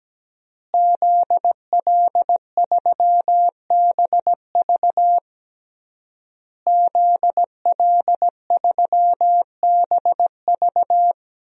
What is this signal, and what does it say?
Morse code, heard as a keyed tone, about 17 wpm, 700 Hz. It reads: ZL3BV ZL3BV
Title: DXCC-100 Call Signs - 5 Characters - Repeated 3x - 17wpm